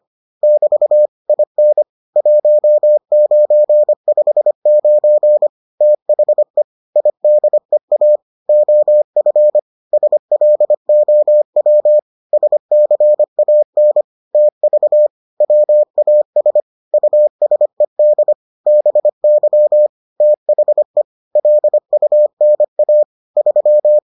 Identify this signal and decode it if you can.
Morse code, heard as a keyed tone, about 25 wpm, 600 Hz. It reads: <BT> IN 1959 THE IDEA OF SLOW SCAN TV WAS USED BY THE LUNA 3